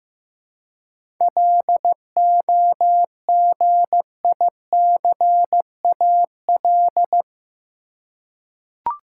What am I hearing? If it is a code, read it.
Morse code, heard as a keyed tone, about 15 wpm, 700 Hz. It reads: LOGICAL
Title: Seven Letter Words - 15wpm